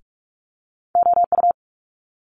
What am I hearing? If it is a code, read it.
Morse code, heard as a keyed tone, about 46 wpm, 700 Hz. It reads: OU
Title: Multi-letter Phonemes Quick Fire 45wpm